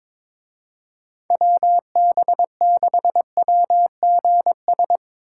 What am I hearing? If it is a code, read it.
Morse code, heard as a keyed tone, about 24 wpm, 700 Hz. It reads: WB6WGS
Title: Speed Racing - USA 2x3 Call Signs 22wpm